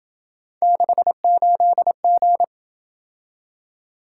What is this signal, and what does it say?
Morse code, heard as a keyed tone, about 27 wpm, 700 Hz. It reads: 68G